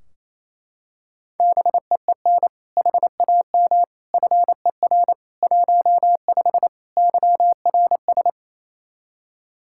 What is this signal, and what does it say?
Morse code, heard as a keyed tone, about 28 wpm, 700 Hz. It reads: BEEN HAM FER 15 YRS
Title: Practice QSO - Exchange 3 - More Info - 28wpm